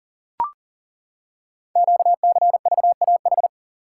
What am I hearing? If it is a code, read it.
Morse code, heard as a keyed tone, about 38 wpm, 700 Hz. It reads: QCVAH
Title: Code Groups - 5 Letters - 40wpm